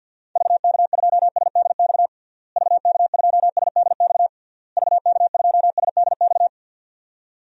Morse code, heard as a keyed tone, about 51 wpm, 700 Hz. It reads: VK2SDX VK2SDX VK2SDX
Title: DXCC-100 Call Signs - 6 Characters - Repeated 3x - 50wpm